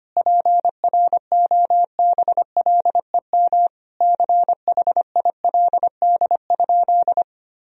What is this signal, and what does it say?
Morse code, heard as a keyed tone, about 25 wpm, 700 Hz. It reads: PROBLEM CHILD?
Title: Sentences from Top 1000 Words - Review Part 2 - 25wpm